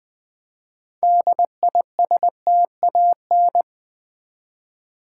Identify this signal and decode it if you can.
Morse code, heard as a keyed tone, about 20 wpm, 700 Hz. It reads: DISTAN
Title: Build A Word - 2 Letters At A Time - 8-Letter Words - 20wpm